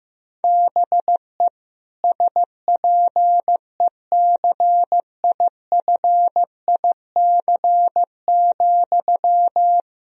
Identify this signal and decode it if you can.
Morse code, heard as a keyed tone, about 15 wpm, 700 Hz. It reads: BE SPECIFIC,
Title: Sentences from Top 1000 Words - Review Part 3 - 15wpm